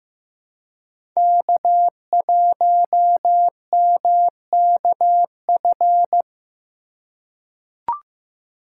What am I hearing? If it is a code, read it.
Morse code, heard as a keyed tone, about 15 wpm, 700 Hz. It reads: K1MKF
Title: USA Call Signs - 5 Characters - 3x Character Spacing - 15wpm